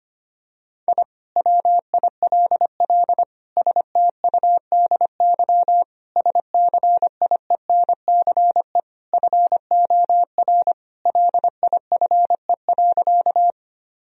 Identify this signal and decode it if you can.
Morse code, heard as a keyed tone, about 25 wpm, 700 Hz. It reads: I WILL STUDY SCIENCE FOR LIFE.
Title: Sentences from Top 1000 Words - Review Part 2 - 25wpm